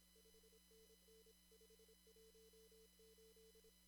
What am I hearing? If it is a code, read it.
Morse code, heard as a keyed tone, about 26 wpm, 440 Hz, with noise. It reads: 5NN 518
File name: CW / kt5j wpx cw-op-k5pi